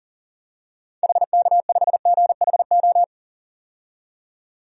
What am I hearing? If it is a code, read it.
Morse code, heard as a keyed tone, about 40 wpm, 700 Hz. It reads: HK5GHO
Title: DXCC-100 Call Signs - 6 Characters - 40wpm